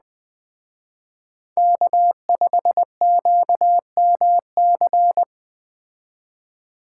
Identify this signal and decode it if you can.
Morse code, heard as a keyed tone, about 20 wpm, 700 Hz. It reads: K5QMC
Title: Speed Racing - USA 1x3 Call Signs 17wpm